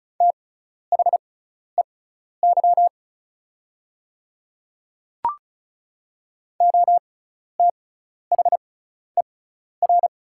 Morse code, heard as a keyed tone, about 36 wpm, spaced as for 12 wpm, 700 Hz. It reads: THEY OTHER
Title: 6x Character Spacing - Top 100 Words - 35wpm